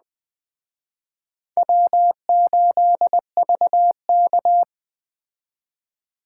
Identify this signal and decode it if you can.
Morse code, heard as a keyed tone, about 20 wpm, 700 Hz. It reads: W8VK